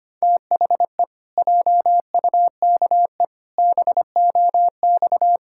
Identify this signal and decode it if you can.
Morse code, heard as a keyed tone, about 25 wpm, 700 Hz. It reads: THE JUKE BOX
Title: Pangram Challenge v2 - 25wpm